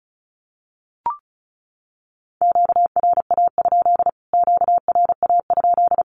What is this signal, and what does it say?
Morse code, heard as a keyed tone, about 34 wpm, 700 Hz. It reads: QRA? QRA?
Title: Q-Codes Repeated 3x 35wpm